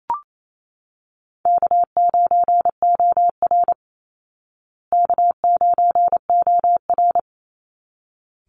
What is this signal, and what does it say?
Morse code, heard as a keyed tone, about 27 wpm, 700 Hz. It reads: K9OR K9OR
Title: Easy Call Signs Repeated 2x - 28wpm